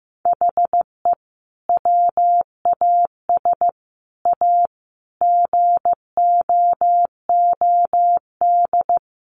Morse code, heard as a keyed tone, about 15 wpm, 700 Hz. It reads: HE WAS A GOOD
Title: Sentences from Top 1300 Words 15wpm